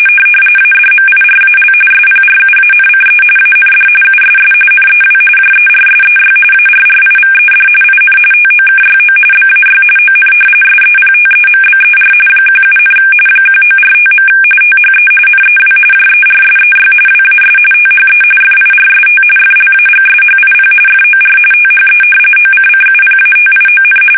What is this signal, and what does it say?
RTTY, about 45.45 baud. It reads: ADMIRAL SPRUANCE AND USS NEW JERSEY ARRIVED IN TOKYO BAY ON SEPTEMBER 17, 1945. THE NEW JERSEY WAS THE FIRST FLAGSHIP FOR THE OCCUPATION
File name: fsk850 / battleshipnj